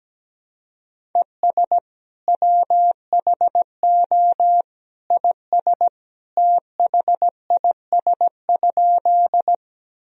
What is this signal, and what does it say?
Morse code, heard as a keyed tone, about 17 wpm, 700 Hz. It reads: ES WHO IS THIS?
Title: Sentences from Top 100 Words and Abbreviations 17wpm